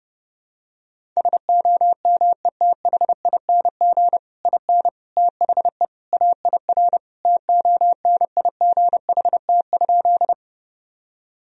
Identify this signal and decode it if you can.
Morse code, heard as a keyed tone, about 32 wpm, 700 Hz. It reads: SOMETHING IN THE AIR TONIGHT?
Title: Sentences from Top 1000 Words - Review Part 1 - 30wpm